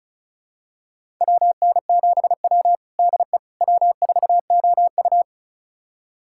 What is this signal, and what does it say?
Morse code, heard as a keyed tone, about 36 wpm, 700 Hz. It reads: WN7W DE W4OU
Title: USA Callsign DE Callsign - 4 Characters - 35wpm